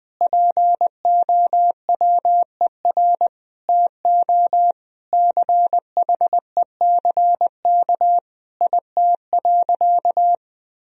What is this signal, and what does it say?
Morse code, heard as a keyed tone, about 21 wpm, 700 Hz. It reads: POWER TO CHECK IT.
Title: Sentences from Top 1000 Words - Review Part 3 - 20wpm